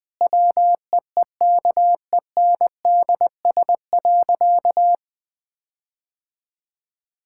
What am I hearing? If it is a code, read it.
Morse code, heard as a keyed tone, about 20 wpm, 700 Hz. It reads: WEEKENDS.
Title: Difficult Sentences with Hints - 20wpm